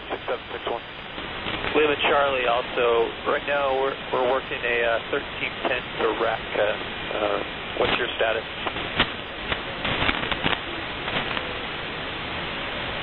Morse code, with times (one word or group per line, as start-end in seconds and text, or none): none